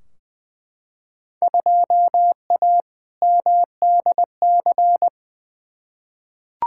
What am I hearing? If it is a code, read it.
Morse code, heard as a keyed tone, about 20 wpm, 700 Hz. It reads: 2A MDC
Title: Field Day Exchanges Encore - 20wpm